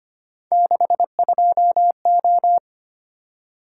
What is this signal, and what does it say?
Morse code, heard as a keyed tone, about 25 wpm, 700 Hz. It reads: 62O
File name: Three Letter-Number Combinations - 25wpm